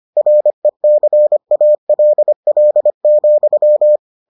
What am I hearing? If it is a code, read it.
Morse code, heard as a keyed tone, about 25 wpm, 600 Hz. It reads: RECALL,